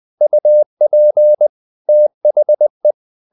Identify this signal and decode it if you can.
Morse code, heard as a keyed tone, about 20 wpm, 600 Hz. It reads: UP THE
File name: the old man and the sea part 2 20wpm